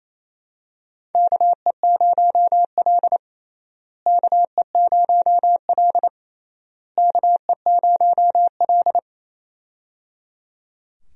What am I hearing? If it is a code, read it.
Morse code, heard as a keyed tone, about 28 wpm, 700 Hz. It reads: KE0L KE0L KE0L
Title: USA Call Signs - 4 Characters - Repeated 3x - 28wpm